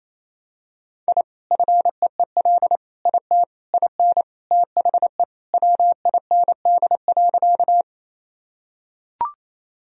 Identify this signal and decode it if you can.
Morse code, heard as a keyed tone, about 27 wpm, 700 Hz. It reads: I FEEL IT IN THE WIND.
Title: Sentences from Top 1000 Words - Review Part 2 - 28wpm